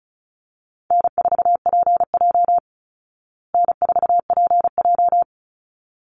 Morse code, heard as a keyed tone, about 34 wpm, 700 Hz. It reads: N4PJ N4PJ
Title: Easy Call Signs Repeated 2x - 35wpm